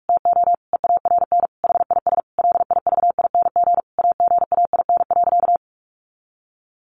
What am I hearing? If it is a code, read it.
Morse code, heard as a keyed tone, about 44 wpm, 700 Hz. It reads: TO EARN HIS LIVING AGAIN.
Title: Sentences from Top 1500 Words 45wpm